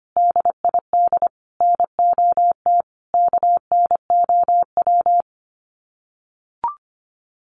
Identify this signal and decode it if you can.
Morse code, heard as a keyed tone, about 25 wpm, 700 Hz. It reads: DID NOT KNOW
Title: Sets of 4 Words 25wpm